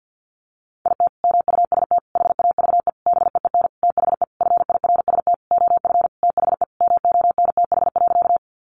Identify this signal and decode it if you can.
Morse code, heard as a keyed tone, about 50 wpm, 700 Hz. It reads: IT MUST HAVE BEEN THE FIRST OF THE MONTH.
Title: Sentences from Top 800 Words 50wpm